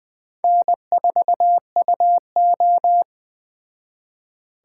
Morse code, heard as a keyed tone, about 20 wpm, 700 Hz. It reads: N4UO
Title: Speed Racing - USA 1x2 Call Signs 17wpm